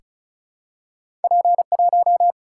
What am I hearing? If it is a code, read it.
Morse code, heard as a keyed tone, about 36 wpm, 700 Hz. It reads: P 1
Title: Single-Letter Single-Number 35wpm